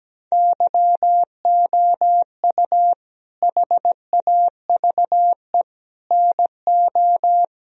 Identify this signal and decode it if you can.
Morse code, heard as a keyed tone, about 17 wpm, 700 Hz. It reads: YOU HAVE NO
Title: Sentences from Top 1000 Words - Review Part 1 - 17wpm